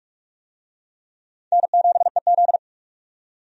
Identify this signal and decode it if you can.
Morse code, heard as a keyed tone, about 46 wpm, 700 Hz. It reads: N7EZ
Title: USA Call Signs - 4 Characters - 45wpm